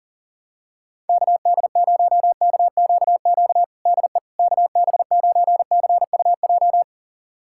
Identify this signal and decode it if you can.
Morse code, heard as a keyed tone, about 40 wpm, 700 Hz. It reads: KD0KQQ DE KB9CUJ
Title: USA Callsign DE Callsign - 6 Characters - 40wpm